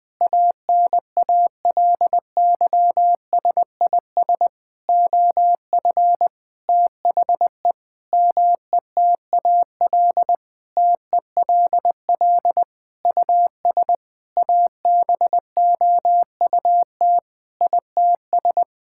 Morse code, heard as a keyed tone, about 21 wpm, 700 Hz. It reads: ANALYSIS OF THE METAL TELL US ABOUT ITS
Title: Sentences from Top 1000 Words - Review Part 6 - 20wpm